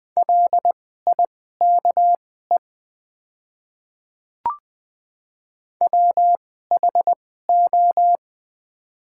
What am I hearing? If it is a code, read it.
Morse code, heard as a keyed tone, about 20 wpm, spaced as for 14 wpm, 700 Hz. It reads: LIKE WHO
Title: 2x Character Spacing - Top 100 Words - 20wpm